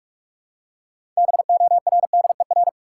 Morse code, heard as a keyed tone, about 46 wpm, 700 Hz. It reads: BORDER
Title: Six Letter Words - 45wpm